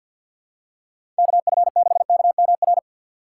Morse code, heard as a keyed tone, about 52 wpm, 700 Hz. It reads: KF6KMR
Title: USA Call Signs - 6 Characters - 6x Character Spacing - 50wpm